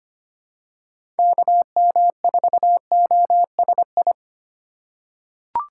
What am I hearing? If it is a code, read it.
Morse code, heard as a keyed tone, about 25 wpm, 700 Hz. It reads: KM4OSI
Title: USA Call Signs - 6 Characters - 2x Character Spacing - 25wpm